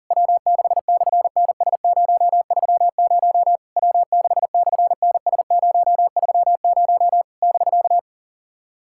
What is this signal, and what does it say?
Morse code, heard as a keyed tone, about 40 wpm, 690 Hz. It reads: W6/NS030 W6/NS030 <BK>
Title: SOTA QSO Practice 40wpm